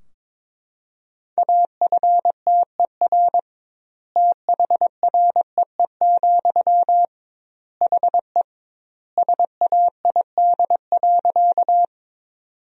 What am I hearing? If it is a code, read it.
Morse code, heard as a keyed tone, about 22 wpm, 700 Hz. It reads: AFTER THREE, HE SAID.
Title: 2x Word-Spacing - Sentences from Top 200 Words - 22wpm